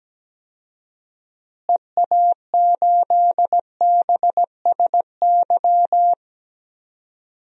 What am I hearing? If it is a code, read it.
Morse code, heard as a keyed tone, about 17 wpm, 700 Hz. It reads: EA8BSY
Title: DXCC-100 Call Signs - 6 Characters - Repeated 2x - 17wpm